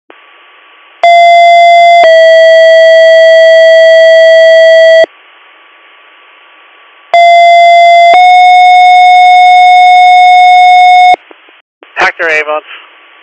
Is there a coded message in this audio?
Yes: two-tone page from 1.0 to 11.1 s, heard as a one-second tone, then a three-second tone, about 690 then 650 Hz, about 690 then 730 Hz, with speech.